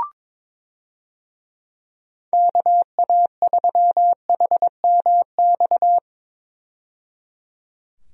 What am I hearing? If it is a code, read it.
Morse code, heard as a keyed tone, about 21 wpm, 700 Hz. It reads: KA3HMX
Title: USA Call Signs - 6 Characters - 22wpm